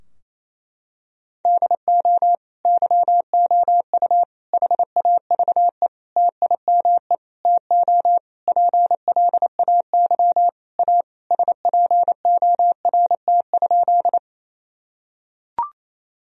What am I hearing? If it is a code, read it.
Morse code, heard as a keyed tone, about 28 wpm, 700 Hz. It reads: DO YOU HAVE TIME TO PLAY A SPORT?
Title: Sentences from Top 1000 Words - Review Part 4 - 28wpm